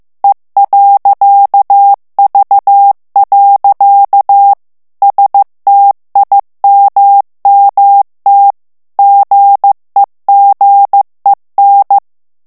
Morse code, heard as a keyed tone, about 15 wpm, 800 Hz. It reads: E.V. STIMMT GEGEN
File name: ALLEMAND texte clair 1 vitesse 14 mots minute